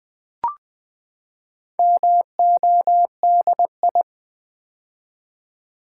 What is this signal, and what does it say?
Morse code, heard as a keyed tone, about 20 wpm, 700 Hz. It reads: MODI
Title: Build A Word - 4 Letters At A Time - 12-Letter Words - 20wpm